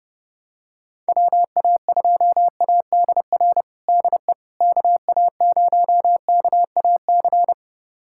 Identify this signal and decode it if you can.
Morse code, heard as a keyed tone, about 30 wpm, 700 Hz. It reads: WA2ADR DE KA0KAC
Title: USA Callsign DE Callsign - 6 Characters - 30wpm